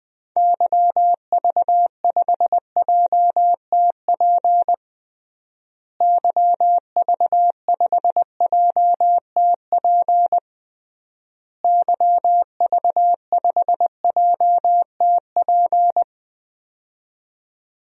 Morse code, heard as a keyed tone, about 21 wpm, 700 Hz. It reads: YV5JTP YV5JTP YV5JTP
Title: DXCC-100 Call Signs - 6 Characters - Repeated 3x - 20wpm